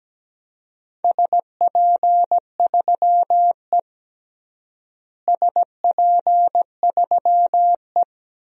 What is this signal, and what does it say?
Morse code, heard as a keyed tone, about 17 wpm, 700 Hz. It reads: SP3E SP3E
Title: DXCC-100 Call Signs - 4 Characters - Repeated 3x - 17wpm